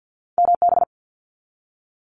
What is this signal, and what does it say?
Morse code, heard as a keyed tone, about 50 wpm, 700 Hz. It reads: MB
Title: Canadian Provinces and Territories 50wpm